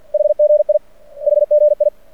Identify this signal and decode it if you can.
Morse code, heard as a keyed tone, about 50 wpm, 600 Hz. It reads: HGI HGI
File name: archived / CRSnet